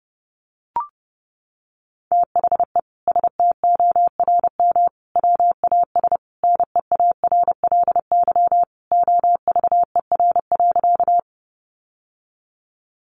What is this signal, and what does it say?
Morse code, heard as a keyed tone, about 30 wpm, 700 Hz. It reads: THE STORM WAS NEARLY OVER.
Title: Sentences from Top 1500 Words 30wpm